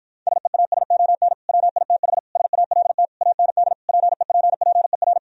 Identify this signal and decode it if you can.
Morse code, heard as a keyed tone, about 52 wpm, 700 Hz. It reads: SEASON WITH SALT AND PEPPER